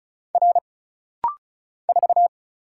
Morse code, heard as a keyed tone, about 36 wpm, 700 Hz. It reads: R 4
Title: Single Letter-Number - Rapid-Fire 35wpm